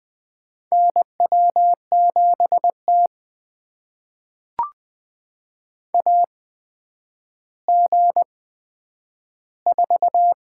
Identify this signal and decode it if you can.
Morse code, heard as a keyed tone, about 20 wpm, spaced as for 10 wpm, 700 Hz. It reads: NW7T AG4
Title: USA Call Signs - 4 Characters - 8x Character Spacing - 20wpm